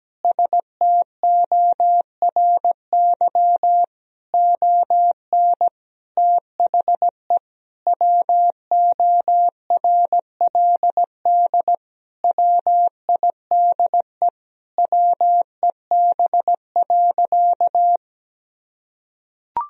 Morse code, heard as a keyed tone, about 17 wpm, 700 Hz. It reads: STORY ON THE WORLD WIDE WEB.
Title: Sentences from Top 1000 Words - Review Part 4 - 17wpm